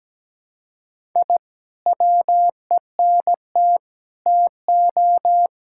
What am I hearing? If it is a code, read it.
Morse code, heard as a keyed tone, about 17 wpm, 700 Hz. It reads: I WENT TO
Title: Sentences from Top 1000 Words - Review Part 6 - 17wpm